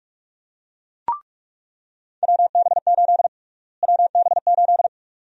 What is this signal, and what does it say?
Morse code, heard as a keyed tone, about 44 wpm, 700 Hz. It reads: WB8 WB8
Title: USA Call Sign Prefixes - Repeated 2x - 45wpm